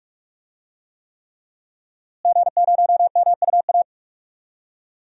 Morse code, heard as a keyed tone, about 47 wpm, 700 Hz. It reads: G0MUA
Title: DXCC-100 Call Signs - 5 Characters - Repeated 3x - 45wpm